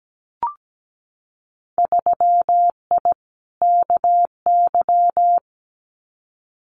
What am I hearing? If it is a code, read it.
Morse code, heard as a keyed tone, about 17 wpm, 700 Hz. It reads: 3I KY
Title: Winter Field Day Exchanges 17wpm